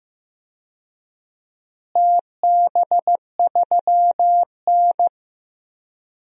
Morse code, heard as a keyed tone, about 15 wpm, 700 Hz. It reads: TB3N
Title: DXCC-100 Call Signs - 4 Characters - Repeated 2x - 15wpm